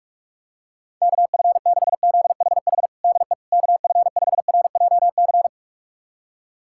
Morse code, heard as a keyed tone, about 41 wpm, 700 Hz. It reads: KF6ZHH DE KF5RJC
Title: USA Callsign DE Callsign 45wpm